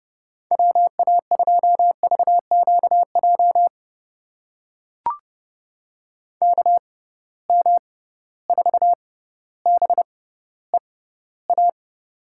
Morse code, heard as a keyed tone, about 30 wpm, spaced as for 15 wpm, 700 Hz. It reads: WA2VQJ KM4BEA
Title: USA Call Signs - 6 Characters - 6x Character Spacing - 30wpm